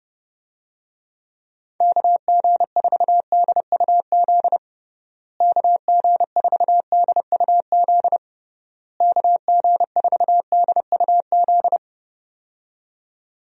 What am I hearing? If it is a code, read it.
Morse code, heard as a keyed tone, about 30 wpm, 700 Hz. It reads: KG4DUZ KG4DUZ KG4DUZ
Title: USA Call Signs - 6 Characters - Repeated 3x - 30wpm